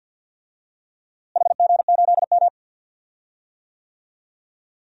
Morse code, heard as a keyed tone, about 53 wpm, 700 Hz. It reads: HG8M
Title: DXCC-100 Call Signs - 4 Characters - Repeated 3x - 50wpm